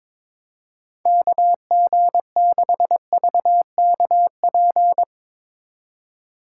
Morse code, heard as a keyed tone, about 22 wpm, 700 Hz. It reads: KG6VKP
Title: USA Call Signs - 6 Characters - 3x Character Spacing - 22wpm